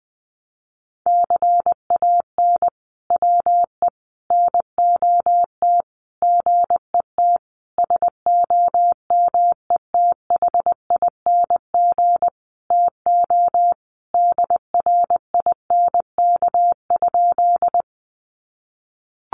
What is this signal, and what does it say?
Morse code, heard as a keyed tone, about 20 wpm, 700 Hz. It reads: CAN WE NOT GET SOMETHING TO DRINK?
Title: Sentences from Top 1500 Words 20wpm